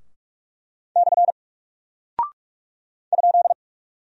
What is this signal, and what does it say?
Morse code, heard as a keyed tone, about 44 wpm, 700 Hz. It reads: / ?
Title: Single Character (Letter:Number:Punctuation:Prosign) 45wpm